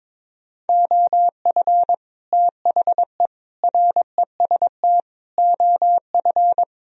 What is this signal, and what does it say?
Morse code, heard as a keyed tone, about 23 wpm, 700 Hz. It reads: OF THE REST OF
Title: Speed-Racing - Sets of 5 Words 22wpm